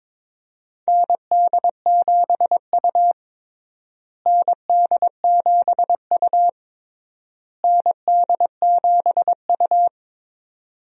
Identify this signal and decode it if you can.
Morse code, heard as a keyed tone, about 22 wpm, 700 Hz. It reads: ND7U ND7U ND7U
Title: USA Call Signs - 4 Characters - Repeated 3x - 22wpm